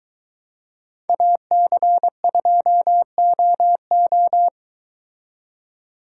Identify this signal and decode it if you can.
Morse code, heard as a keyed tone, about 22 wpm, 700 Hz. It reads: AC2OO